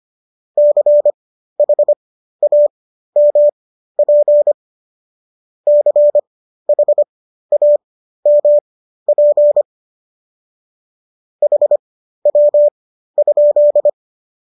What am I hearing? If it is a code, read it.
Morse code, heard as a keyed tone, about 25 wpm, spaced as for 13 wpm, 590 Hz. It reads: CHAMP CHAMP HW?